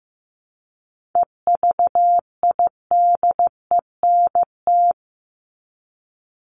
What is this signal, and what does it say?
Morse code, heard as a keyed tone, about 15 wpm, 700 Hz. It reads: EVIDENT
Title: LY Suffix - Top 100 Words with Base Words 15wpm